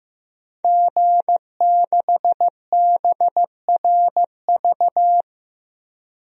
Morse code, heard as a keyed tone, about 16 wpm, 700 Hz. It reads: G6BRV
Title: DXCC-100 Call Signs - 5 Characters - 15wpm